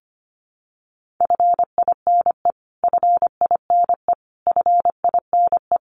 Morse code, heard as a keyed tone, about 25 wpm, 700 Hz. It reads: FINE FINE FINE
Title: Top 500 Words Repeated 25wpm